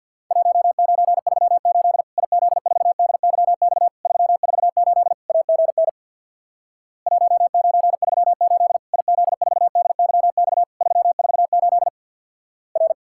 Morse code, heard as a keyed tone, about 50 wpm, 700 Hz. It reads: 1938 IZ4DYX 348 AGN 1938 IZ4DYX 348 R
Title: Worked All Europe DX Contest - QTC Exchange with Repeats - 50wpm